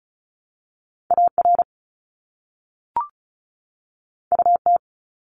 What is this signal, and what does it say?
Morse code, heard as a keyed tone, about 36 wpm, 700 Hz. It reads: AR UT
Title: ARRL Sections 35wpm